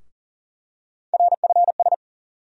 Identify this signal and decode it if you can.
Morse code, heard as a keyed tone, about 40 wpm, 700 Hz. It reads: RFS
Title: USA Call Sign Postfixes - 3 Letters 40wpm